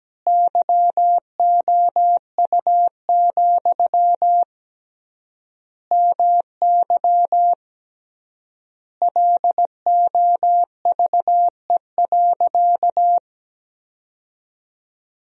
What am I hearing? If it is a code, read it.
Morse code, heard as a keyed tone, about 17 wpm, 700 Hz. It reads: YOU, MY LOVE.
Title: 3x Word-Spacing - Sentences from Top 400 Words - 17wpm